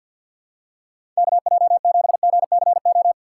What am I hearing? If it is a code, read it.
Morse code, heard as a keyed tone, about 48 wpm, 700 Hz. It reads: KJ7GCO
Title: USA Call Signs - 6 Characters - 6x Character Spacing - 50wpm